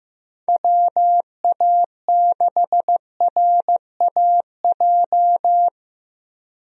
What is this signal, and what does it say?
Morse code, heard as a keyed tone, about 15 wpm, 700 Hz. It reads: WA6RAJ
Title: USA Call Signs - 6 Characters - Repeated 2x - 15wpm